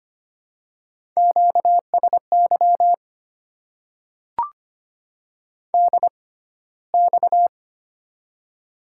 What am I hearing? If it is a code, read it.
Morse code, heard as a keyed tone, about 25 wpm, spaced as for 15 wpm, 700 Hz. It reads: QSY DX
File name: Top 90 QSO Elements - 6x Character Spacing - 25wpm